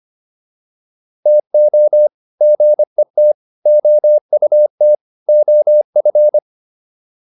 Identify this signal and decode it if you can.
Morse code, heard as a keyed tone, about 25 wpm, 600 Hz. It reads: TO GET OUT OF